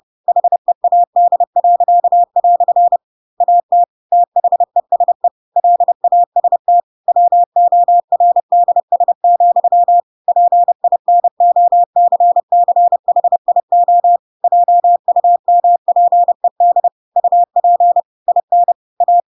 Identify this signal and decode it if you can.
Morse code, heard as a keyed tone, about 30 wpm, 700 Hz. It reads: HEAD." AT THESE LAST WORDS, PINOCCHIO JUMPED UP IN A
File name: Pinocchio / Pinocchio ch4 30 30 ik1hge